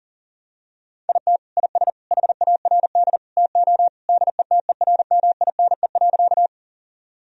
Morse code, heard as a keyed tone, about 40 wpm, 700 Hz. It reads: IT IS HARD TO DETERMINE.
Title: Sentences from Top 1000 Words - Review Part 4 - 40wpm